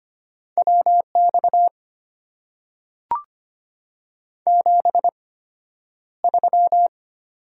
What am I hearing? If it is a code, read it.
Morse code, heard as a keyed tone, about 25 wpm, spaced as for 11 wpm, 690 Hz. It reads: WX 73
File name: Top 90 QSO Elements - 8x Character Spacing - 25wpm